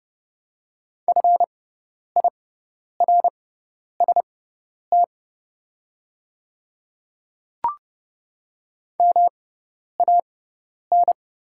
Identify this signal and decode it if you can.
Morse code, heard as a keyed tone, about 30 wpm, spaced as for 10 wpm, 700 Hz. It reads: FIRST MAN
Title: 6x Character Spacing - Top 300 Words - 30wpm